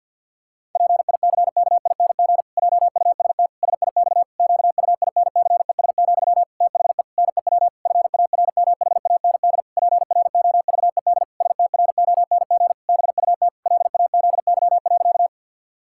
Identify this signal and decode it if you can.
Morse code, heard as a keyed tone, about 50 wpm, 700 Hz. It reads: PICKING JUST SIX QUINCES, THE NEW FARMHAND PROVED STRONG BUT LAZY.
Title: Pangram Challenge v2 - 50wpm